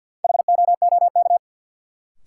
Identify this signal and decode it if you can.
Morse code, heard as a keyed tone, about 47 wpm, 700 Hz. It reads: HOOK
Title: Sentences from Top 1600 Words - New Word Drill Alt - 50wpm